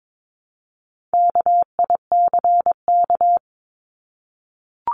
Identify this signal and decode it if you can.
Morse code, heard as a keyed tone, about 22 wpm, 700 Hz. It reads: KICK
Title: Top 1251-1500 Words 22wpm